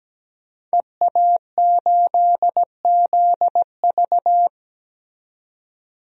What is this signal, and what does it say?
Morse code, heard as a keyed tone, about 17 wpm, 700 Hz. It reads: EA8ZV
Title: DXCC-100 Call Signs - 5 Characters - 17wpm